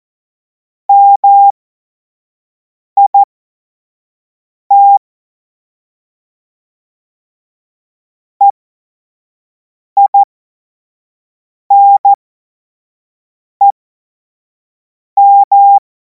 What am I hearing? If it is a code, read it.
Morse code, heard as a keyed tone, about 14 wpm, spaced as for 5 wpm, 800 Hz. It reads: MIT EINEM